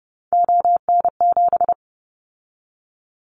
Morse code, heard as a keyed tone, about 30 wpm, 700 Hz. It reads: ON7
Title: Callsign Prefixes for Top 50 DXCC - 30wpm